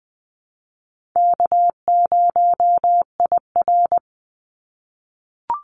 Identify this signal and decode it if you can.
Morse code, heard as a keyed tone, about 20 wpm, 700 Hz. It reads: K0IR
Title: Call Signs - 20wpm